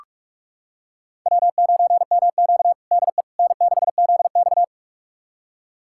Morse code, heard as a keyed tone, about 46 wpm, 700 Hz. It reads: W9MQ DE N6ZX
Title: USA Callsign DE Callsign - 4 Characters - 45wpm